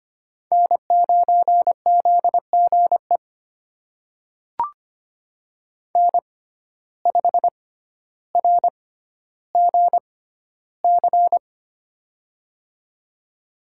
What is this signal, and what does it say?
Morse code, heard as a keyed tone, about 25 wpm, spaced as for 13 wpm, 700 Hz. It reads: N9ZGE N5RGC
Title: USA Call Signs - 6x Character Spacing - 25wpm